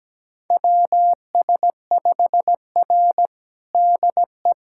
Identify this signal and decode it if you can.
Morse code, heard as a keyed tone, about 17 wpm, 700 Hz. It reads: WS5R DE
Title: USA Callsign DE Callsign - 4 Characters - 17wpm